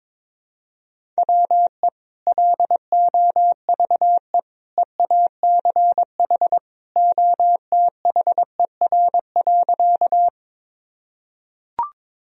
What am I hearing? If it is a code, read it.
Morse code, heard as a keyed tone, about 22 wpm, 700 Hz. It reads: WE LOVE EACH OTHER.
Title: Sentences from Top 1000 Words - Review Part 1 - 22wpm